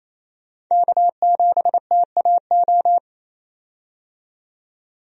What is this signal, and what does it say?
Morse code, heard as a keyed tone, about 28 wpm, 700 Hz. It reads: K7TAO
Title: USA Call Signs - 5 Characters - 28wpm